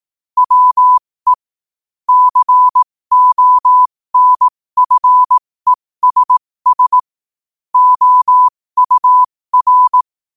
Morse code, heard as a keyed tone, about 18 wpm, spaced as for 15 wpm, 1000 Hz. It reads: WE CONFESS OUR
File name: qotdfiles / 15wpm